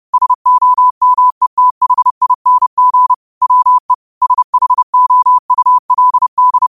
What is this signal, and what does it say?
Morse code, heard as a keyed tone, about 30 wpm, 1000 Hz. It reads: SOMETHING WE SHOULD
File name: qotdfiles / 30wpm